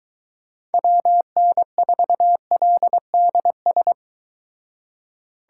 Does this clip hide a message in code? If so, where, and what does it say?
Yes: Morse code from 0.7 to 3.9 s, heard as a keyed tone, about 22 wpm, 700 Hz. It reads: WN4LDS